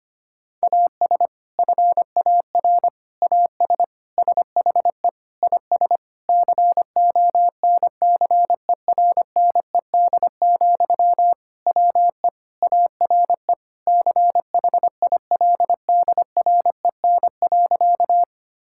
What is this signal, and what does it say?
Morse code, heard as a keyed tone, about 25 wpm, 700 Hz. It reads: AS FAR AS SHE IS CONCERNED, WE ARE CHILDREN.
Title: Sentences from Top 1000 Words - Review Part 6 - 25wpm